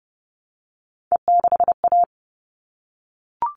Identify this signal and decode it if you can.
Morse code, heard as a keyed tone, about 30 wpm, 700 Hz. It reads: E6A
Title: All Cut Numbers - 3-Digit 30wpm